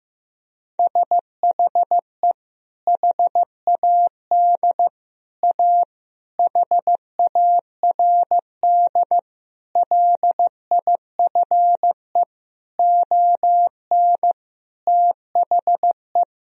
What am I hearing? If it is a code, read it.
Morse code, heard as a keyed tone, about 16 wpm, 700 Hz. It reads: SHE HAD A HARD LIFE ON THE
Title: Sentences from Top 1000 Words - Review Part 2 - 15wpm